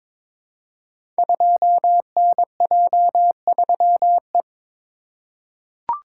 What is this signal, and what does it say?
Morse code, heard as a keyed tone, about 21 wpm, 700 Hz. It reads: 2NJ3E
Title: Code Groups - 5 Letter:Numbers - 22wpm